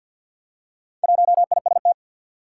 Morse code, heard as a keyed tone, about 47 wpm, 700 Hz. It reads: 1IST
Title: Four Challenging Letter-Number Combinations 50wpm